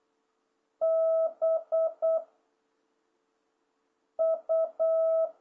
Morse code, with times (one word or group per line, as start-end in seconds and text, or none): 0.8-2.2 B
4.2-5.3 U